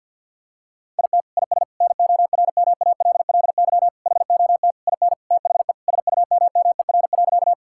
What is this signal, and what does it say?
Morse code, heard as a keyed tone, about 49 wpm, 700 Hz. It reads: IT IS NORMALLY HOT IN THE SUMMER.
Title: Sentences from Top 1000 Words - Review Part 6 - 50wpm